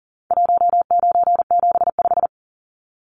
Morse code, heard as a keyed tone, about 40 wpm, 700 Hz. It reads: 1975
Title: Numbers - 4-Digit 40wpm